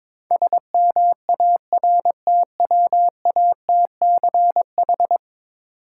Morse code, heard as a keyed tone, about 22 wpm, 700 Hz. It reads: SMARTWATCH
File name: Difficult Words with Hints - 22wpm